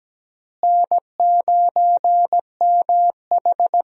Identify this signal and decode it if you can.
Morse code, heard as a keyed tone, about 17 wpm, 700 Hz. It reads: N9MH
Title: USA Call Signs - 4 Characters - Repeated 3x - 17wpm